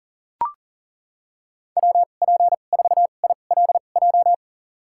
Morse code, heard as a keyed tone, about 40 wpm, spaced as for 29 wpm, 700 Hz. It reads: WP4ILJ
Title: USA Call Signs - 6 Characters - 2x Character Spacing - 40wpm